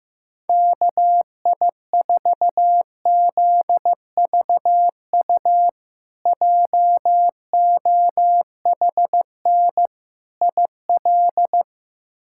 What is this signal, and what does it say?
Morse code, heard as a keyed tone, about 16 wpm, 700 Hz. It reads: KI4ZVU JOHN IL
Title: K1USN SST QSO Practice - 15wpm